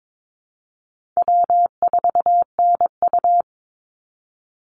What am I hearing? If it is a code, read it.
Morse code, heard as a keyed tone, about 22 wpm, 700 Hz. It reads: W4NU
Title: Easy Call Signs Repeated 3x - 22wpm